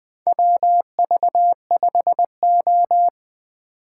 Morse code, heard as a keyed tone, about 21 wpm, 700 Hz. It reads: WV5O
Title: USA Call Signs - 4 Characters - 4x Character Spacing - 20wpm